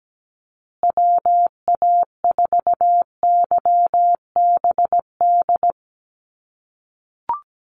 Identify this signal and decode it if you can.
Morse code, heard as a keyed tone, about 17 wpm, 700 Hz. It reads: WA4YBD
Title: USA Call Signs Repeated 3x 17wpm